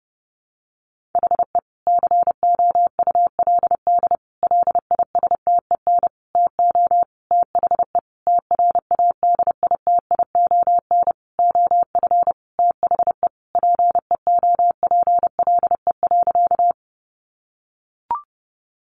Morse code, heard as a keyed tone, about 30 wpm, 700 Hz. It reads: HE COULD LISTEN TO THE TRADITION OF THE PEOPLE.
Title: Sentences from Top 1500 Words 30wpm